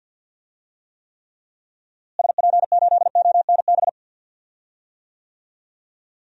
Morse code, heard as a keyed tone, about 52 wpm, 700 Hz. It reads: SP8ONB
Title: DXCC-100 Call Signs - 6 Characters - Repeated 3x - 50wpm